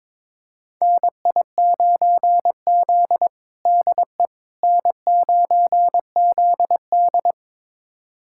Morse code, heard as a keyed tone, about 22 wpm, 700 Hz. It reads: NI9Z DE N9ZD
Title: USA Callsign DE Callsign - 4 Characters - 22wpm